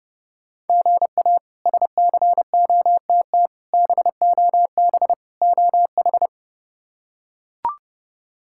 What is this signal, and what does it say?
Morse code, heard as a keyed tone, about 30 wpm, 700 Hz. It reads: GA SCOTT BOB OH
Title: K1USN SST QSO Practice - 30wpm